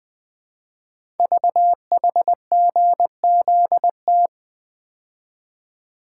Morse code, heard as a keyed tone, about 20 wpm, 700 Hz. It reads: VHGZT
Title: Code Groups - 5 Letters - 20wpm